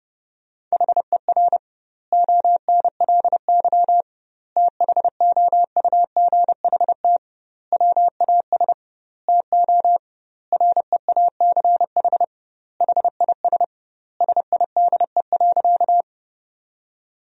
Morse code, heard as a keyed tone, about 30 wpm, 700 Hz. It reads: HER ONLY THOUGHT WAS TO REACH HIS SIDE.
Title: 2x Word-Spacing - Sentences from Top 400 Words - 30wpm